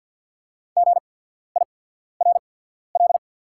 Morse code, heard as a keyed tone, about 48 wpm, spaced as for 14 wpm, 690 Hz. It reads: GIRL
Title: 8x Character Spacing - Top 500 Words - 50wpm